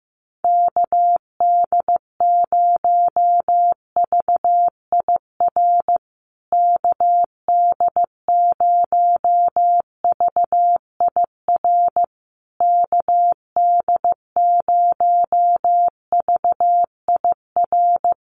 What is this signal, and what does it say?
Morse code, heard as a keyed tone, about 15 wpm, 700 Hz. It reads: KD0VIR KD0VIR KD0VIR
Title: USA Call Signs Repeated 3x 15wpm